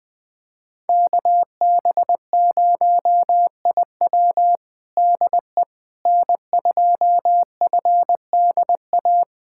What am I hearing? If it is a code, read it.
Morse code, heard as a keyed tone, about 20 wpm, 700 Hz. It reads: KB0IW DE N2FDA
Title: USA Callsign DE Callsign 20wpm